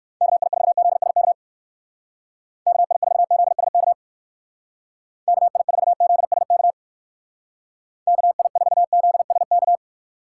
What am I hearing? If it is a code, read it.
Morse code, heard as a keyed tone, about 56 wpm, 700 Hz. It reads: KI4ZSK KI4ZSK KI4ZSK KI4ZSK
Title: Speed Racing - USA 2x3 Call Signs 45wpm